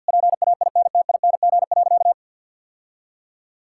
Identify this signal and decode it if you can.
Morse code, heard as a keyed tone, about 50 wpm, 700 Hz. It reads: PAINTING.
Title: Sentences from Top 1000 Words - Review Part 6 - 50wpm